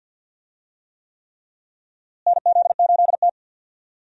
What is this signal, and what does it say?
Morse code, heard as a keyed tone, about 52 wpm, 700 Hz. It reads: NZ8T
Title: USA Call Signs - 4 Characters - Repeated 3x - 50wpm